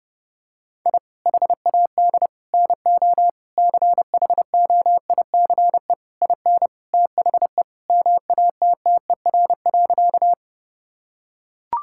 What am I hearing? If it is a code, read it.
Morse code, heard as a keyed tone, about 30 wpm, 700 Hz. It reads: I HAD NO CHOICE IN THE MATTER.
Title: Sentences from Top 1000 Words - Review Part 6 - 30wpm